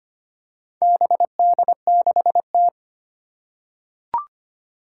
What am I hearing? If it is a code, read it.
Morse code, heard as a keyed tone, about 25 wpm, 700 Hz. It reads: BD6T
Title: Four Challenging Letter-Number Combinations 25wpm